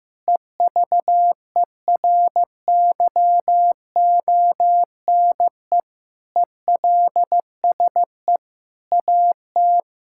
Morse code, heard as a keyed tone, about 15 wpm, 700 Hz. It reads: EVERYONE ELSE AT
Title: Sentences from Top 1600 Words 15wpm